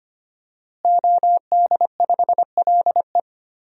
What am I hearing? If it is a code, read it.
Morse code, heard as a keyed tone, about 25 wpm, 700 Hz. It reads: OD5LE
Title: DXCC-100 Call Signs - 5 Characters - Repeated 2x - 25wpm